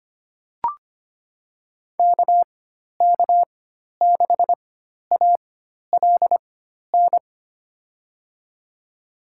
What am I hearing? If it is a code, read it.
Morse code, heard as a keyed tone, about 25 wpm, spaced as for 12 wpm, 700 Hz. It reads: KK6ALN
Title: USA Call Signs - 6 Characters - 4x Character Spacing - 25wpm